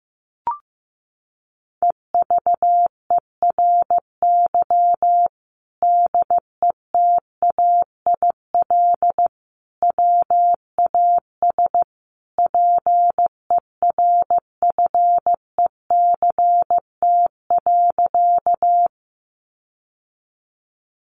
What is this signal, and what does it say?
Morse code, heard as a keyed tone, about 15 wpm, 700 Hz. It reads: EVERY DETAIL WAS PERFECT.
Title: Sentences from Top 1500 Words 15wpm